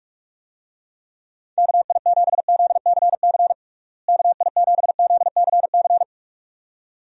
Morse code, heard as a keyed tone, about 42 wpm, 700 Hz. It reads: KI7ZCC KI7ZCC
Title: USA Call Signs - 6 Characters - Repeated 2x - 45wpm